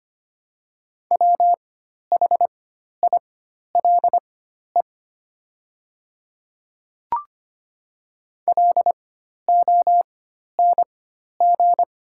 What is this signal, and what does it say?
Morse code, heard as a keyed tone, about 25 wpm, spaced as for 12 wpm, 700 Hz. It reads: WHILE LONG
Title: 4x Character Spacing - Top 300 Words - 25wpm